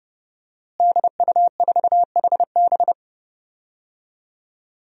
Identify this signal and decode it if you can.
Morse code, heard as a keyed tone, about 30 wpm, 700 Hz. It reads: DU4HB
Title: DXCC-100 Call Signs - 5 Characters - Repeated 3x - 30wpm